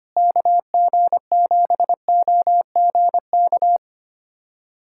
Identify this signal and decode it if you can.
Morse code, heard as a keyed tone, about 25 wpm, 700 Hz. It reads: KG7OGK
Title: USA Callsign DE Callsign 25wpm